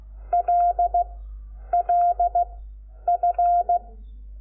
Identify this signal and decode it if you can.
Morse code, heard as a keyed tone, about 16 wpm, 700 Hz. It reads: L L F
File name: Lecciones / CLASE Nº5